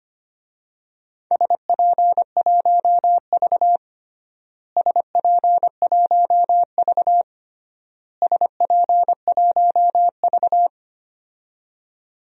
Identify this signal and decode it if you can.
Morse code, heard as a keyed tone, about 25 wpm, 700 Hz. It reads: SP1V SP1V SP1V
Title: DXCC-100 Call Signs - 4 Characters - Repeated 3x - 25wpm